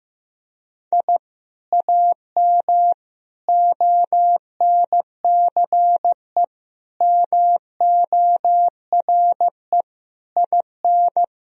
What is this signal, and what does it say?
Morse code, heard as a keyed tone, about 16 wpm, 700 Hz. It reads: I AM ONCE MORE IN
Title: Sentences from Top 1000 Words - Review Part 3 - 15wpm